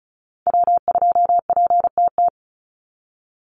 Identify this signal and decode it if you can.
Morse code, heard as a keyed tone, about 34 wpm, 700 Hz. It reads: W2PTT
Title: Call Signs Repeated 3x - 35wpm